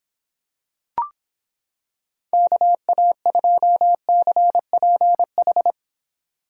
Morse code, heard as a keyed tone, about 26 wpm, 700 Hz. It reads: KA2CPH